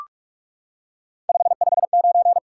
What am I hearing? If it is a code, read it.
Morse code, heard as a keyed tone, about 44 wpm, 700 Hz. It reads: 559
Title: Top 90 QSO Elements 45wpm